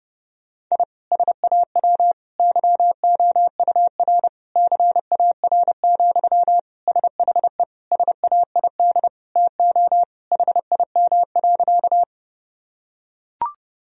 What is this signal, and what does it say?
Morse code, heard as a keyed tone, about 30 wpm, 700 Hz. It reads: I SAW YOUR CAR, SHE SAID TO HIM.
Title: Sentences from Top 1000 Words - Review Part 1 - 30wpm